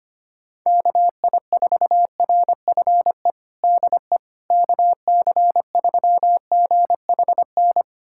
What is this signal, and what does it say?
Morse code, heard as a keyed tone, about 25 wpm, 700 Hz. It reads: KI4RFE DE KC3GHN
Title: USA Callsign DE Callsign - 6 Characters - 25wpm